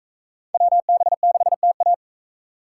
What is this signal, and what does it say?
Morse code, heard as a keyed tone, about 40 wpm, 700 Hz. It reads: WB6TA